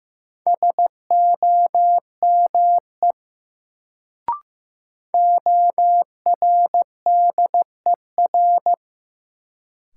Morse code, heard as a keyed tone, about 15 wpm, 700 Hz. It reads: SOME ORDER
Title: Sets of 2 Words - New Word Drill - 15wpm